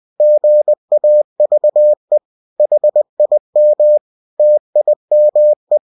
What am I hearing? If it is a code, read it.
Morse code, heard as a keyed tone, about 20 wpm, 600 Hz. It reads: GAVE HIM TIME
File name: cw-books / winnie-the-pooh / winnie-20wpm-page-0101